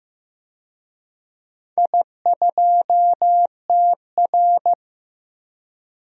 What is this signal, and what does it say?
Morse code, heard as a keyed tone, about 15 wpm, 700 Hz. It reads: I2TR
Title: DXCC-100 Call Signs - 4 Characters - 15wpm